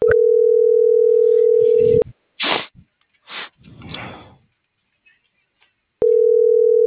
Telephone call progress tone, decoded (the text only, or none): ringback tone